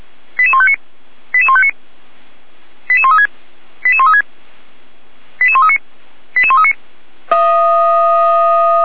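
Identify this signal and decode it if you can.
Five-tone selective call (ZVEI), heard as quick runs of five tones, tones of about 70 ms. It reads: 88159 88159 88136 88136 88139 88139
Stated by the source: ZVEI five-tone call 88397 received on an SDR.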